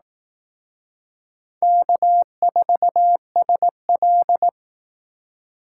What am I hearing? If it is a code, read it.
Morse code, heard as a keyed tone, about 18 wpm, 700 Hz. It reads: K4SL
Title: Speed Racing - USA 1x2 Call Signs 15wpm